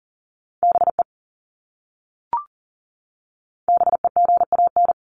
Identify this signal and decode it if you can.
Morse code, heard as a keyed tone, about 40 wpm, 700 Hz. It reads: BE BEGAN
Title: Top 500 Words Repeated 40wpm